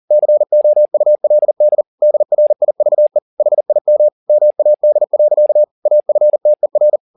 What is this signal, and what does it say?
Morse code, heard as a keyed tone, about 40 wpm, 600 Hz. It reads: COULD DRIVE HIM MAD. AFTER
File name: the old man and the sea 2 40wpm